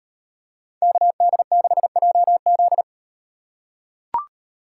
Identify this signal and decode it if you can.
Morse code, heard as a keyed tone, about 38 wpm, 700 Hz. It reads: KD6JZ